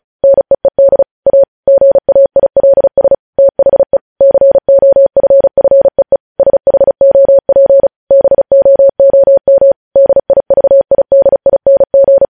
Morse code, heard as a keyed tone, about 36 wpm, 550 Hz. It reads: NEED A GAILS THE COFFEE SHOP BOOM DIVIDING
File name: podcast / News headlines-1x-35x35